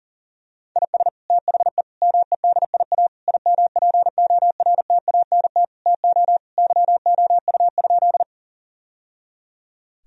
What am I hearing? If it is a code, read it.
Morse code, heard as a keyed tone, about 40 wpm, 700 Hz. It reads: IS THE MEDIA IMPORTANT TO YOU?
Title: Sentences from Top 1000 Words - Review Part 4 - 40wpm